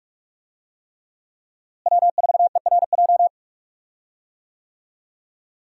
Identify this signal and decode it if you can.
Morse code, heard as a keyed tone, about 43 wpm, 700 Hz. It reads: W4ERJ
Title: USA Call Signs - 5 Characters - Repeated 3x - 45wpm